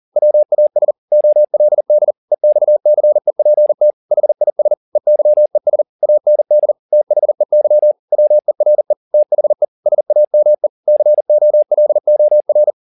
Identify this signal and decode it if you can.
Morse code, heard as a keyed tone, about 40 wpm, 600 Hz. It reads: WAS OLD EXCEPT HIS EYES AND THEY WERE THE SAME COLOR
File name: the old man and the sea 1 40wpm